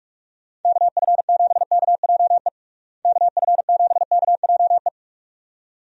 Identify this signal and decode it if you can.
Morse code, heard as a keyed tone, about 46 wpm, 700 Hz. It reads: KF7KJE KF7KJE
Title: USA Call Signs - 6 Characters - Repeated 2x - 45wpm